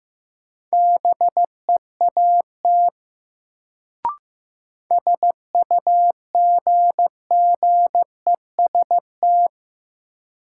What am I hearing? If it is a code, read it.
Morse code, heard as a keyed tone, about 15 wpm, 700 Hz. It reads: BEAT SUGGEST
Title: Sentences from Top 1300 Words - New Word Drill Alt - 15wpm